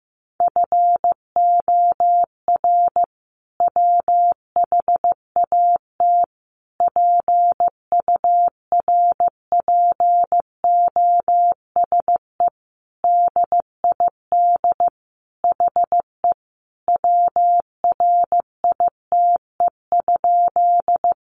Morse code, heard as a keyed tone, about 15 wpm, 700 Hz. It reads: FOR WHAT PURPOSE DID HE WRITE?
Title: Sentences from Top 800 Words 15wpm